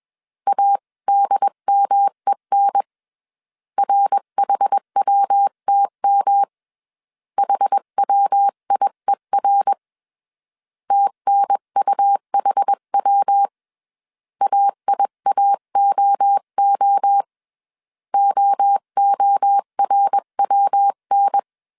Morse code, heard as a keyed tone, about 21 wpm, 800 Hz. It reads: ADMEN RHWTM HWIER TNUHW AIAOO OORWN
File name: CW / DET / 1 3 1500C S B 20wpm 1754